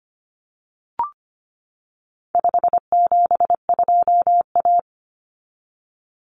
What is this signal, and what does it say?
Morse code, heard as a keyed tone, about 25 wpm, 700 Hz. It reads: 572A
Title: Common Cut Numbers - 4-Digit 25wpm